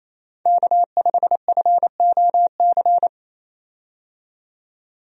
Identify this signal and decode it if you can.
Morse code, heard as a keyed tone, about 27 wpm, 700 Hz. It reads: K5FOC
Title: USA Call Signs - 5 Characters - 28wpm